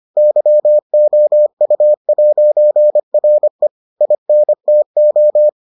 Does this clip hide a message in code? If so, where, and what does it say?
Yes: Morse code from 0.2 to 5.5 s, heard as a keyed tone, about 25 wpm, 600 Hz. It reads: YOU'RE INTO